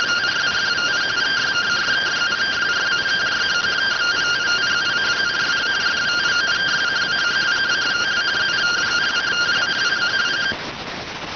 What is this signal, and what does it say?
Two-tone teleprinter signal, about 170 Hz shift.